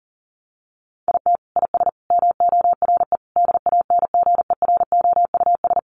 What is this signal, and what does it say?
Morse code, heard as a keyed tone, about 40 wpm, 700 Hz. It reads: IT IS MORE DANGEROUS
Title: Sentences from Top 1200 Words 40wpm